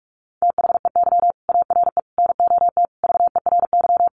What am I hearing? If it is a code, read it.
Morse code, heard as a keyed tone, about 44 wpm, 700 Hz. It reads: THEY ARE NOT VERY
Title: Sentences from Top 1400 Words 45wpm